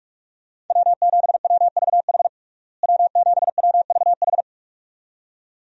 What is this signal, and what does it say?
Morse code, heard as a keyed tone, about 46 wpm, 700 Hz. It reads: W7WVH W7WVH
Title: USA Call Signs - 5 Characters - Repeated 2x - 45wpm